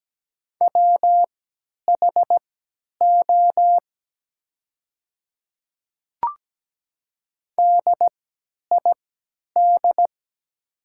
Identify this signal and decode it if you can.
Morse code, heard as a keyed tone, about 17 wpm, spaced as for 10 wpm, 700 Hz. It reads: WHO DID
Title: 3x Character Spacing - Top 100 Words - 17wpm